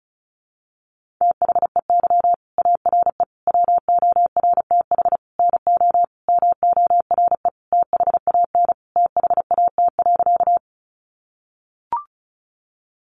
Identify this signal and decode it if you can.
Morse code, heard as a keyed tone, about 34 wpm, 700 Hz. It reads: THEY ARE WORTH NO MORE THAN THAT.
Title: Sentences from Top 800 Words 35wpm